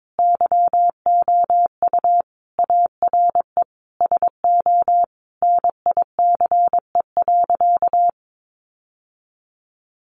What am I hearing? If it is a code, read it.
Morse code, heard as a keyed tone, about 22 wpm, 700 Hz. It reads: YOU ARE SO NICE.
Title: Sentences from Top 1100 Words 22wpm